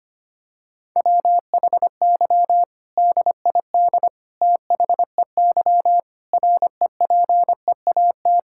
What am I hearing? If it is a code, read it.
Morse code, heard as a keyed tone, about 25 wpm, 700 Hz. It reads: WHY DID THEY REPEAT
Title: Sentences from Top 1600 Words 25wpm